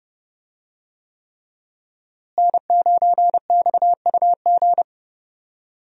Morse code, heard as a keyed tone, about 29 wpm, 700 Hz. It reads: N9XUG
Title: USA Call Signs - 5 Characters - Repeated 3x - 30wpm